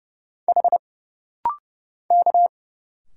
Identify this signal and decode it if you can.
Morse code, heard as a keyed tone, about 30 wpm, 700 Hz. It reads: H K
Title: Single Letters - Rapid-Fire 30wpm